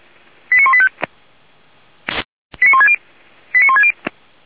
Five-tone selective call (ZVEI), heard as quick runs of five tones, tones of about 70 ms. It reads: 89185 89160 89160